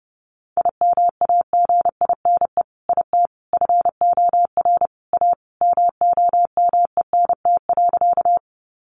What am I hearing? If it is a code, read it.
Morse code, heard as a keyed tone, about 28 wpm, 700 Hz. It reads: IMAGINE IT FOR A MOMENT.
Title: Sentences from Top 1400 Words 30wpm